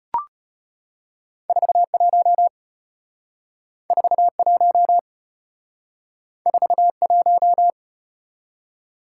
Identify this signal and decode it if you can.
Morse code, heard as a keyed tone, about 33 wpm, 700 Hz. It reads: 41 41 41